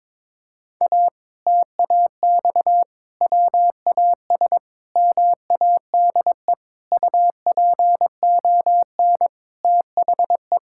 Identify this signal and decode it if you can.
Morse code, heard as a keyed tone, about 22 wpm, 700 Hz. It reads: A TAX WAS MADE UPON THE
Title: Sentences from Top 1000 Words - Review Part 6 - 22wpm